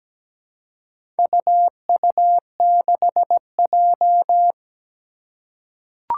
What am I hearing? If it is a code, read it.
Morse code, heard as a keyed tone, about 17 wpm, 700 Hz. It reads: UU6J
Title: Four Challenging Letter-Number Combinations 17wpm